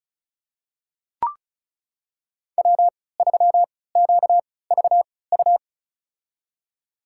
Morse code, heard as a keyed tone, about 36 wpm, spaced as for 20 wpm, 700 Hz. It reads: W3QVU
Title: USA Call Signs - 5 Characters - 3x Character Spacing - 35wpm